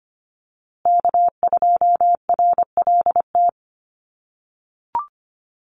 Morse code, heard as a keyed tone, about 25 wpm, 700 Hz. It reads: K2RLT
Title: USA Call Signs Repeated 2x 25wpm